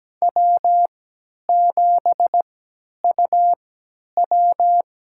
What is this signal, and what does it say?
Morse code, heard as a keyed tone, about 17 wpm, spaced as for 10 wpm, 700 Hz. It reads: W7UW
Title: USA Call Signs - 4 Characters - 3x Character Spacing - 17wpm